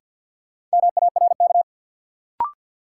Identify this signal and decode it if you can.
Morse code, heard as a keyed tone, about 53 wpm, 700 Hz. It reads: MARK
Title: Sentences from Top 300 Words - New Word Drill - 50wpm